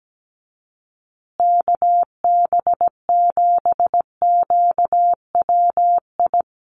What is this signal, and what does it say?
Morse code, heard as a keyed tone, about 17 wpm, 700 Hz. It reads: KB7QWI
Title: USA Call Signs Repeated 3x 17wpm